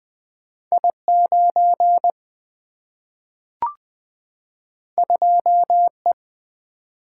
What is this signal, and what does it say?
Morse code, heard as a keyed tone, about 20 wpm, 700 Hz. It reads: I 9 2 E
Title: Single-Letter Single-Number 20wpm